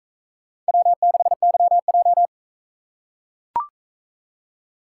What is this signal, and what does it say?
Morse code, heard as a keyed tone, about 42 wpm, 700 Hz. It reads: W6YJ